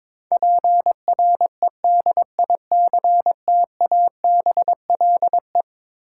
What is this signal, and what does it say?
Morse code, heard as a keyed tone, about 22 wpm, 700 Hz. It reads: PREDICTABLE
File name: practice / 22wpm / Eleven Letter Words - 22wpm